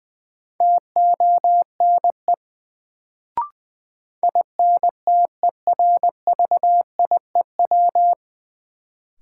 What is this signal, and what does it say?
Morse code, heard as a keyed tone, about 21 wpm, 700 Hz. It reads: TONE INTERVIEW
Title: Sentences from Top 1600 Words - New Word Drill - 20wpm